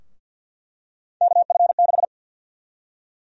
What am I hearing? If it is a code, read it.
Morse code, heard as a keyed tone, about 46 wpm, 700 Hz. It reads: KF6
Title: USA Call Sign Prefixes - Repeated 2x - 50wpm